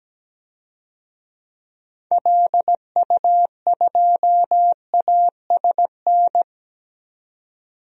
Morse code, heard as a keyed tone, about 17 wpm, 700 Hz. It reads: LU2ASN
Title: DXCC-100 Call Signs - 6 Characters - 17wpm